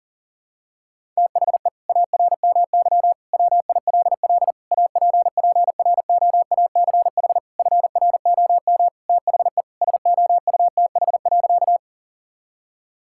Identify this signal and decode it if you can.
Morse code, heard as a keyed tone, about 40 wpm, 700 Hz. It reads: THE ARMY WILL APPROACH FROM THE SOUTH.
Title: Sentences from Top 1000 Words - Review Part 6 - 40wpm